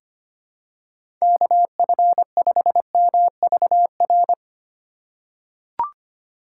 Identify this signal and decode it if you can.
Morse code, heard as a keyed tone, about 25 wpm, 700 Hz. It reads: KF5MVR
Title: USA Call Signs - 6 Characters - 2x Character Spacing - 25wpm